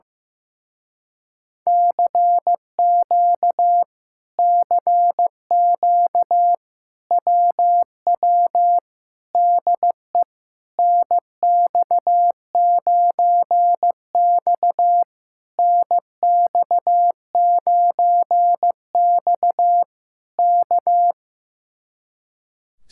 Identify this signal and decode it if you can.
Morse code, heard as a keyed tone, about 15 wpm, 700 Hz. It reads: CQ CQ WW DE NX9X NX9X K
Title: Speed-Racing Common QSO Phrases Encore - 15wpm